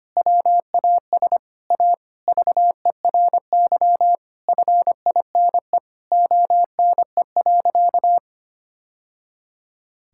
Morse code, heard as a keyed tone, about 25 wpm, 700 Hz. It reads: WAS A VERY FINE ONE.
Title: Sentences from Top 1000 Words - Review Part 6 - 25wpm